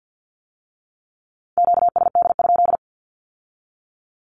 Morse code, heard as a keyed tone, about 50 wpm, 700 Hz. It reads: QSD?
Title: Q-Codes with Spoken Meaning - 50wpm